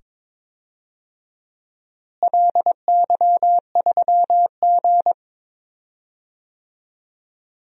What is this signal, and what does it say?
Morse code, heard as a keyed tone, about 22 wpm, 700 Hz. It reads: LY3G
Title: DXCC-100 Call Signs - 4 Characters - Repeated 2x - 22wpm